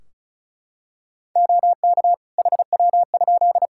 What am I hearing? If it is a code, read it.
Morse code, heard as a keyed tone, about 34 wpm, 700 Hz. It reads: OK HW?
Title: Practice QSO - Exchange 2 - Info - 35wpm